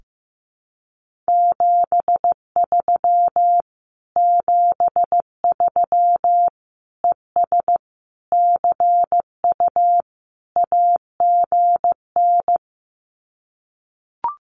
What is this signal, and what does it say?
Morse code, heard as a keyed tone, about 15 wpm, 700 Hz. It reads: 73 73 ES CU AGN
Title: Common QSO Phrases Encore 15wpm